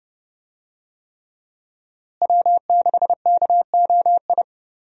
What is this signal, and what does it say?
Morse code, heard as a keyed tone, about 30 wpm, 700 Hz. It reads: W6KOI
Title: USA Call Signs - 5 Characters - Repeated 3x - 30wpm